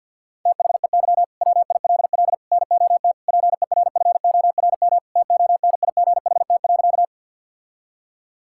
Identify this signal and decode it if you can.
Morse code, heard as a keyed tone, about 47 wpm, 700 Hz. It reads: THEY WILL NOT PERFORM TONIGHT.
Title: Sentences from Top 1000 Words - Review Part 5 - 50wpm